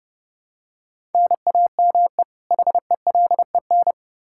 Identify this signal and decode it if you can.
Morse code, heard as a keyed tone, about 28 wpm, 700 Hz. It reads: NAME HELEN
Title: Practice QSO - Exchange 1 - Beginning - 30wpm